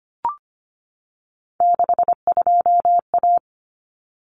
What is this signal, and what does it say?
Morse code, heard as a keyed tone, about 25 wpm, 700 Hz. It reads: 62A
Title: Common Cut Numbers - 3-Digit 25wpm